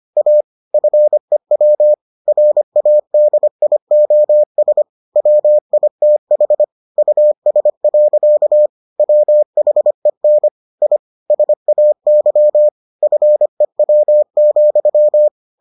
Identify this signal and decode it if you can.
Morse code, heard as a keyed tone, about 25 wpm, 600 Hz. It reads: A FEW RADIOS WITH US. WHEN I SAY FEW,